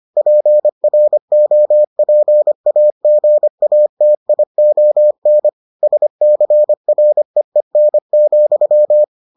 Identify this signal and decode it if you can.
Morse code, heard as a keyed tone, about 25 wpm, 600 Hz. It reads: PROPAGATION SCREEN,